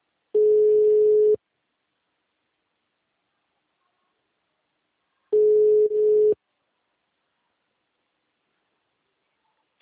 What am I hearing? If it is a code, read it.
Ringback tone (Europe, 425 Hz).